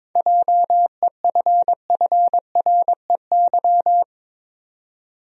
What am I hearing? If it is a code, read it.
Morse code, heard as a keyed tone, about 22 wpm, 700 Hz. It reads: JEFFREY
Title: USA Common Names - 22wpm